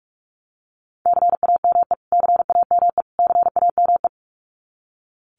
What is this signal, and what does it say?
Morse code, heard as a keyed tone, about 44 wpm, 700 Hz. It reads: CAME CAME CAME
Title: Top 500 Words Repeated 45wpm